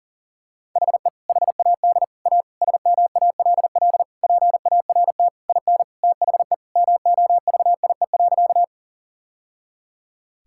Morse code, heard as a keyed tone, about 40 wpm, 700 Hz. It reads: HE HAD A SMALL PART IN THE MOVIE.
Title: Sentences from Top 1000 Words - Review Part 5 - 40wpm